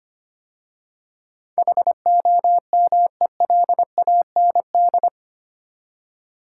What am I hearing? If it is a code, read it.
Morse code, heard as a keyed tone, about 25 wpm, 690 Hz. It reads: HOMELAND
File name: 8 Letter Words - First 3 Letters Given - 25wpm